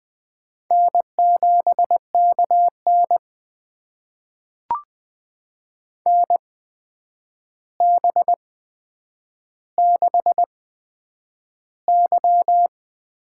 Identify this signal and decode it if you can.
Morse code, heard as a keyed tone, about 20 wpm, spaced as for 9 wpm, 700 Hz. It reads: N7KN NB6Y
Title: USA Call Signs - 4 Characters - 8x Character Spacing - 20wpm